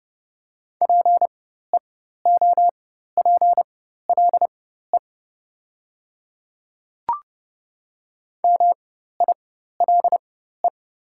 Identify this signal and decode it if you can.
Morse code, heard as a keyed tone, about 30 wpm, spaced as for 14 wpm, 700 Hz. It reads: PEOPLE MILE
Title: 4x Character Spacing - Top 300 Words - 30wpm